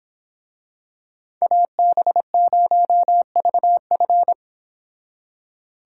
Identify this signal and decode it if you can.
Morse code, heard as a keyed tone, about 26 wpm, 700 Hz. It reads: AB0VF
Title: Speed Racing - USA 2x2 Call Signs 22wpm